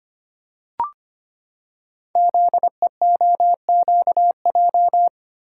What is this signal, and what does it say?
Morse code, heard as a keyed tone, about 25 wpm, 700 Hz. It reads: ZEOQJ
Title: Code Groups - 5 Letters - 25wpm